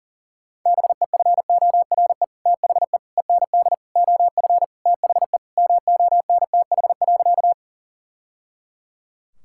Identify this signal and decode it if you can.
Morse code, heard as a keyed tone, about 40 wpm, 700 Hz. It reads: BEFORE THE END OF THE MONTH.
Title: Sentences from Top 1000 Words - Review Part 5 - 40wpm